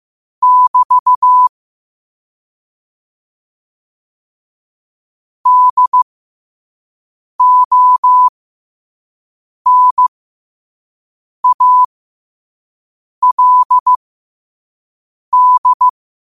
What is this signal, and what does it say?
Morse code, heard as a keyed tone, about 15 wpm, spaced as for 5 wpm, 1000 Hz. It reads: <BT> DONALD
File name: qotdfiles / 5wpm